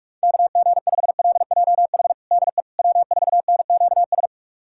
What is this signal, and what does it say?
Morse code, heard as a keyed tone, about 48 wpm, 700 Hz. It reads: KG5LJH DE W4NQS
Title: USA Callsign DE Callsign 45wpm